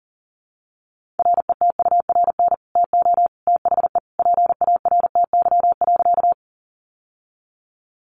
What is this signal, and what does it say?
Morse code, heard as a keyed tone, about 40 wpm, 700 Hz. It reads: RETURN TO THE PARTY.
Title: Sentences from Top 1400 Words 40wpm